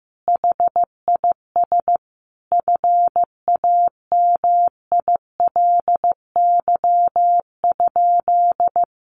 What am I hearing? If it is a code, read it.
Morse code, heard as a keyed tone, about 15 wpm, 700 Hz. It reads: HIS FAMILY?
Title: Sentences from Top 1300 Words 15wpm